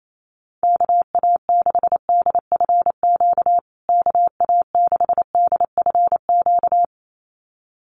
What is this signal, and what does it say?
Morse code, heard as a keyed tone, about 28 wpm, 700 Hz. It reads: KA6DFQ KA6DFQ
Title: USA Call Signs Repeated 2x 28wpm